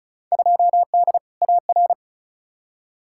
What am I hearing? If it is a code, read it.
Morse code, heard as a keyed tone, about 36 wpm, 700 Hz. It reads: 2D AR
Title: Field Day Exchanges Encore - 35wpm